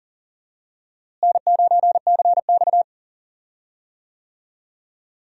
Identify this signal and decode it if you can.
Morse code, heard as a keyed tone, about 40 wpm, 700 Hz. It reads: N9CX
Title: USA Call Signs - 4 Characters - Repeated 2x - 40wpm